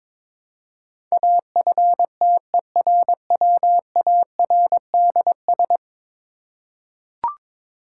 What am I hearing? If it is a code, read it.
Morse code, heard as a keyed tone, about 22 wpm, 700 Hz. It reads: AFTERWARDS
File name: practice / 22wpm / 10 Letter Words - First 3 Letters Given - 22wpm